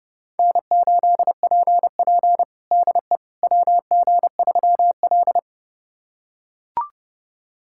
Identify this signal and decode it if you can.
Morse code, heard as a keyed tone, about 32 wpm, 700 Hz. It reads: N8PP DE WG3L
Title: USA Callsign DE Callsign - 4 Characters - 30wpm